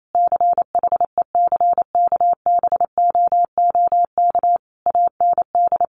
Morse code, heard as a keyed tone, about 28 wpm, 700 Hz. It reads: CHECKBOOK AND
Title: A Sound Of Thunder 28wpm